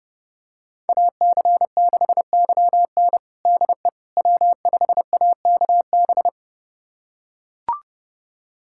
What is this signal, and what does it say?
Morse code, heard as a keyed tone, about 30 wpm, 700 Hz. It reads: AC6YN DE W5AKB
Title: USA Callsign DE Callsign - 5 Characters - 30wpm